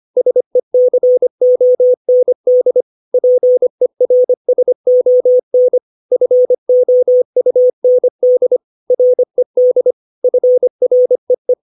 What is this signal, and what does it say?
Morse code, heard as a keyed tone, about 25 wpm, 500 Hz. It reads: SECOND PERSON FOUND RED FREE
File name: 632 words / 632 Words 25WPM